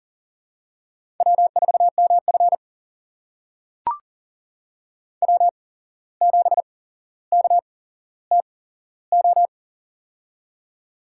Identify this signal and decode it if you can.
Morse code, heard as a keyed tone, about 42 wpm, spaced as for 16 wpm, 700 Hz. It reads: W4MF W7KTO
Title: USA Call Signs - 8x Character Spacing - 40wpm